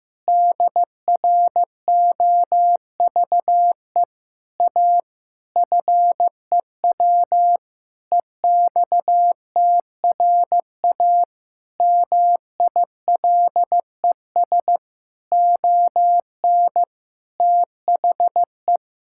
Morse code, heard as a keyed tone, about 15 wpm, 700 Hz. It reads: DROVE A FEW EXTRA MILES ON THE
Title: Pangram Challenge v2 - 15wpm